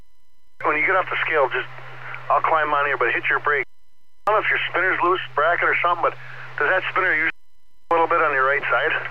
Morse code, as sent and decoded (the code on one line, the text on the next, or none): none